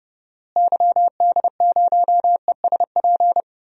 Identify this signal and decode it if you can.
Morse code, heard as a keyed tone, about 30 wpm, 700 Hz. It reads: YD0ESP
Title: DXCC-100 Call Signs - 6 Characters - Repeated 2x - 30wpm